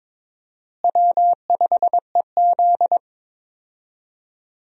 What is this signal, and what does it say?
Morse code, heard as a keyed tone, about 21 wpm, 700 Hz. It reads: W5EZ
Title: USA Call Signs - 4 Characters - Repeated 3x - 22wpm